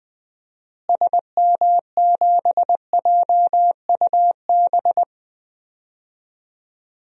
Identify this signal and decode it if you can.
Morse code, heard as a keyed tone, about 20 wpm, 700 Hz. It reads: SM7JUB
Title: DXCC-100 Call Signs - 6 Characters - Repeated 2x - 20wpm